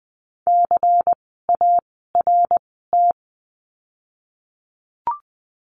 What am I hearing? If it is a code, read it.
Morse code, heard as a keyed tone, about 20 wpm, spaced as for 15 wpm, 700 Hz. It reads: CART
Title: 094 Lesson 16f - Letter c - Words 20wpm